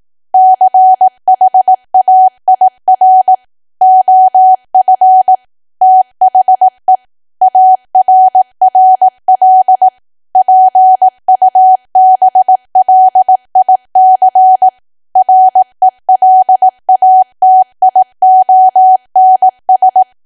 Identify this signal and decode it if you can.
Morse code, heard as a keyed tone, about 18 wpm, 750 Hz. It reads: CHAIR OF THE ARRL PUBLIC RELATIONS
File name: Archive / 18 WPM / 18WPM